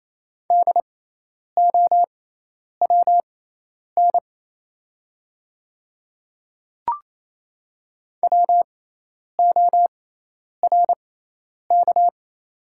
Morse code, heard as a keyed tone, about 28 wpm, spaced as for 10 wpm, 700 Hz. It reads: DOWN WORK
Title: 6x Character Spacing - Top 200 Words - 28wpm